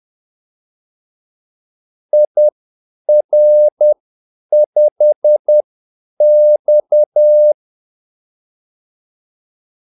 Morse code, heard as a keyed tone, about 10 wpm, spaced as for 8 wpm, 600 Hz. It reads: IR5X
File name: Callsigns-10-8wpm